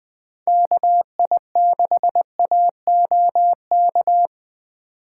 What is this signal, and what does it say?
Morse code, heard as a keyed tone, about 20 wpm, 700 Hz. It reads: KI6AOK
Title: Speed Racing - USA 2x3 Call Signs 17wpm